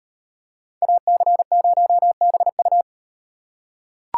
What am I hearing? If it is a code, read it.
Morse code, heard as a keyed tone, about 38 wpm, 700 Hz. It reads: AC0BU